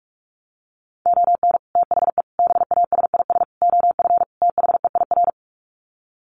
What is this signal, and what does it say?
Morse code, heard as a keyed tone, about 44 wpm, 700 Hz. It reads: ON THE BASIS OF THEIR
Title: Sets of 5 Words Encore 45wpm